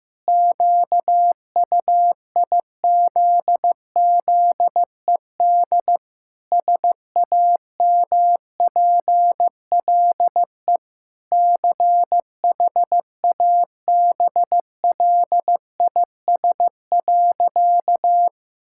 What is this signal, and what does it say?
Morse code, heard as a keyed tone, about 15 wpm, 700 Hz. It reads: QUIZZED SAMPLE CHABLIS.
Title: Pangram Challenge v2 - 15wpm